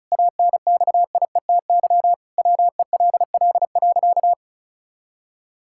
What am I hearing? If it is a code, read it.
Morse code, heard as a keyed tone, about 36 wpm, 700 Hz. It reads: ANXIETY WELL.
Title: Sentences from Top 1600 Words 35wpm